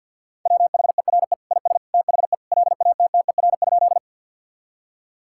Morse code, heard as a keyed tone, about 52 wpm, 700 Hz. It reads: WHERE IS THE LATTER?
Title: Sentences from Top 1600 Words 50wpm